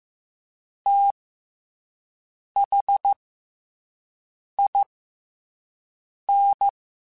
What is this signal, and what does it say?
Morse code, heard as a keyed tone, about 15 wpm, 800 Hz. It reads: T H I N
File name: Special INGs-mp3s / Special INGs-mp3s / SlowToFaster ing @10